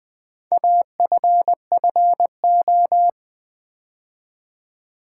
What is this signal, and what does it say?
Morse code, heard as a keyed tone, about 20 wpm, 700 Hz. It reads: AFFO
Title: Build A Word - 4 Letters At A Time - 10-Letter Words - 20wpm